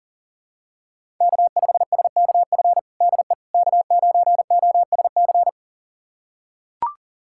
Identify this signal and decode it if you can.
Morse code, heard as a keyed tone, about 40 wpm, 700 Hz. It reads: K5SKF DE K9OSC
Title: USA Callsign DE Callsign - 5 Characters - 40wpm